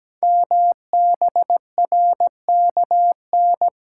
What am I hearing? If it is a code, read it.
Morse code, heard as a keyed tone, about 17 wpm, 700 Hz. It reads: MBRKN
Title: Code Groups - 5 Letters - 17wpm